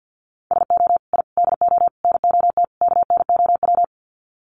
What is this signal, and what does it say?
Morse code, heard as a keyed tone, about 50 wpm, 700 Hz. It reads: SO I DO NOT KNOW
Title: Sets of 5 Words Encore 50wpm